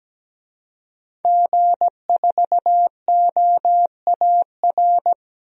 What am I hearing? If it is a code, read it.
Morse code, heard as a keyed tone, about 17 wpm, 700 Hz. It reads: G4OAR
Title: DXCC-100 Call Signs - 5 Characters - 17wpm